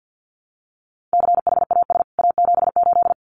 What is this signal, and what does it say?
Morse code, heard as a keyed tone, about 50 wpm, 700 Hz. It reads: CHAS A78
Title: CWT Exchange 50wpm